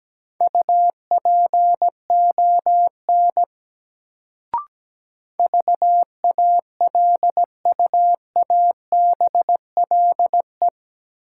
Morse code, heard as a keyed tone, about 17 wpm, 700 Hz. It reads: UPON VALUABLE
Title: Sentences from Top 1100 Words - New Word Drill Alt - 17wpm